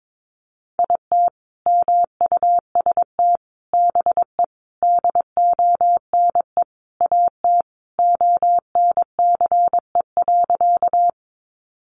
Morse code, heard as a keyed tone, about 22 wpm, 700 Hz. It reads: IT MUST BE DONE AT ONCE.
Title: Sentences from Top 500 Words 22wpm